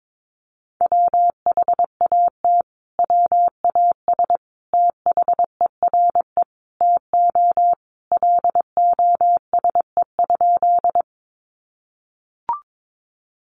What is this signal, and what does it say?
Morse code, heard as a keyed tone, about 22 wpm, 700 Hz. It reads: WHAT WAS THERE TO LOSE?
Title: Sentences from Top 1100 Words 22wpm